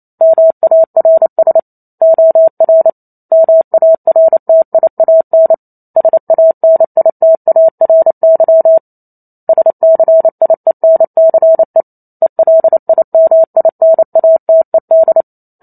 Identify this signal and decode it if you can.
Morse code, heard as a keyed tone, about 28 wpm, 650 Hz. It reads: MARS OR MARTIAN SANITARY SCIENCE ELIMINATED